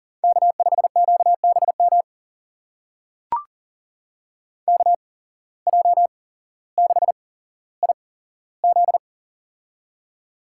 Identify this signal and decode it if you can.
Morse code, heard as a keyed tone, about 40 wpm, spaced as for 17 wpm, 700 Hz. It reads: K5QBM KJ6IZ
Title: USA Call Signs - 5 Characters - 8x Character Spacing - 40wpm